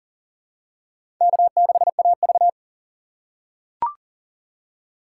Morse code, heard as a keyed tone, about 40 wpm, 700 Hz. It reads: K6AV
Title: USA Call Signs - 4 Characters - 3x Character Spacing - 40wpm